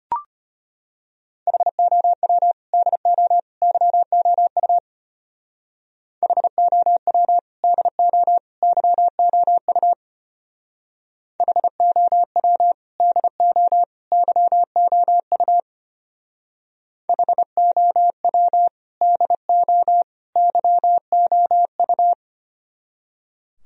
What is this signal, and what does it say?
Morse code, heard as a keyed tone, about 32 wpm, 700 Hz. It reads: HOW DO YOU HOW DO YOU HOW DO YOU HOW DO YOU